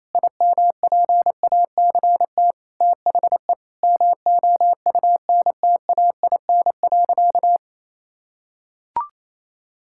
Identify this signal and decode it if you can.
Morse code, heard as a keyed tone, about 28 wpm, 700 Hz. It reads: IMPACT THE MOUNTAIN.
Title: Sentences from Top 1000 Words - Review Part 6 - 28wpm